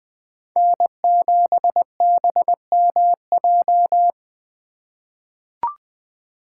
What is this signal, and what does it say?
Morse code, heard as a keyed tone, about 20 wpm, 700 Hz. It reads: N7BMJ
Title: USA Call Signs - 5 Characters - 3x Character Spacing - 20wpm